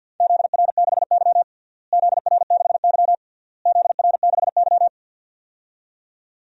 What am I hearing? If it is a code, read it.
Morse code, heard as a keyed tone, about 46 wpm, 700 Hz. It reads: ZR6Y ZR6Y ZR6Y
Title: DXCC-100 Call Signs - 4 Characters - Repeated 3x - 50wpm